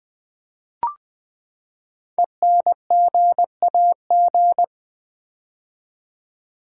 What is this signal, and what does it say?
Morse code, heard as a keyed tone, about 20 wpm, 700 Hz. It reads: ENGAG
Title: Build A Word - 5 Letters At A Time - 10-Letter Words - 20wpm